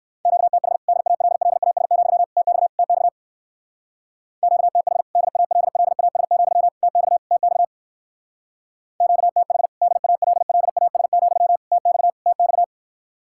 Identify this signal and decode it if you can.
Morse code, heard as a keyed tone, about 57 wpm, 700 Hz. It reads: QTH DALLAS, TX TX QTH DALLAS, TX TX QTH DALLAS, TX TX